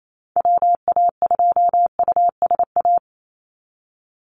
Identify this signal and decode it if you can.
Morse code, heard as a keyed tone, about 28 wpm, 700 Hz. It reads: WA2USA
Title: NA SOTA Chaser Callsigns 28wpm